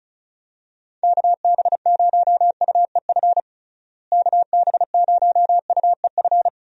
Morse code, heard as a keyed tone, about 34 wpm, 700 Hz. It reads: KB0UEF KB0UEF
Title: USA Call Signs - 6 Characters - Repeated 2x - 35wpm